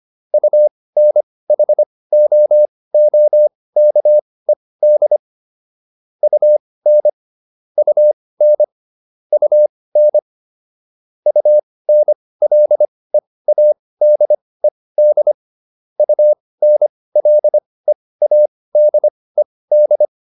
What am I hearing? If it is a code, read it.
Morse code, heard as a keyed tone, about 25 wpm, spaced as for 18 wpm, 600 Hz. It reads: UNHOOKED UN UN UN UNLEADED UNLEADED